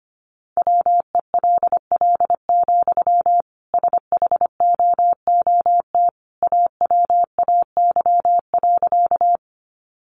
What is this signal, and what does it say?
Morse code, heard as a keyed tone, about 25 wpm, 700 Hz. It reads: WELL, SHOOT AWAY.
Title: Sentences from Top 1200 Words 25wpm